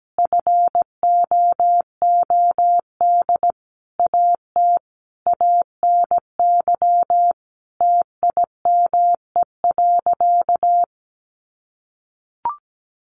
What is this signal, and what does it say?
Morse code, heard as a keyed tone, about 17 wpm, 700 Hz. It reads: FOOD AT ANY TIME.
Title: Sentences from Top 1100 Words 17wpm